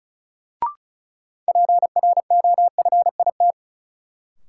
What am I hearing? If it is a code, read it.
Morse code, heard as a keyed tone, about 36 wpm, 700 Hz. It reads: PROFIT
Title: Sentences from Top 800 Words - New Word Drill - 35wpm